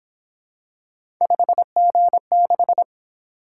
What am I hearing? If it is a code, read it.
Morse code, heard as a keyed tone, about 26 wpm, 700 Hz. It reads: 5G6
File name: Speed-Racing - Three Letter-Number Combinations - 22wpm